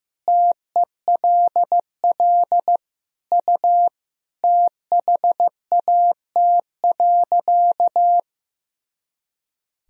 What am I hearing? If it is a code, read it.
Morse code, heard as a keyed tone, about 15 wpm, 700 Hz. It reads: TELL U THAT.
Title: Sentences from Top 300 Words and Abbreviations 15wpm